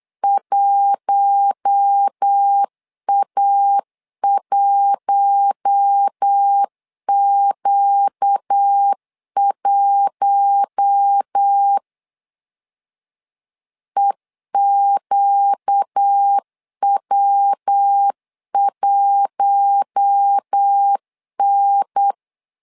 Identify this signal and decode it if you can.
Morse code, heard as a keyed tone, about 8 wpm, 800 Hz. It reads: 1A1Q1 EQW1N
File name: LEZIONI / 4 Q1 8wpm 3035